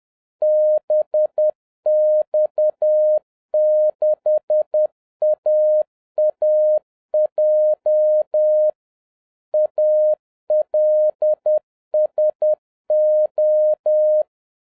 Morse code, heard as a keyed tone, about 10 wpm, 610 Hz. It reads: BX6AAJ ALSO
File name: 2011-10-mp3 / n01-CALLSIGN ASSIGNED 10wpm